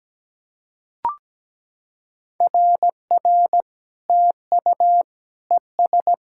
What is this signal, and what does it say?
Morse code, heard as a keyed tone, about 17 wpm, 700 Hz. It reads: RR TU ES
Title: SOTA QSO Practice 17wpm